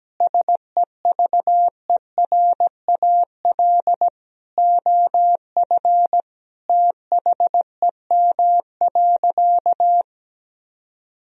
Morse code, heard as a keyed tone, about 17 wpm, 700 Hz. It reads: SEVERAL OF THEM.
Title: Sentences from Top 1000 Words - Review Part 2 - 17wpm